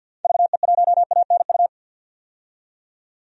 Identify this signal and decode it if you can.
Morse code, heard as a keyed tone, about 51 wpm, 700 Hz. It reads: VE1ANU
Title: DXCC-100 Call Signs - 6 Characters - 50wpm